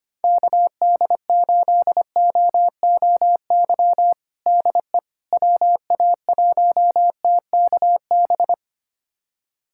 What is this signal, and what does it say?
Morse code, heard as a keyed tone, about 25 wpm, 700 Hz. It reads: KD8OOY DE WA1TKB
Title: USA Callsign DE Callsign - 6 Characters - 25wpm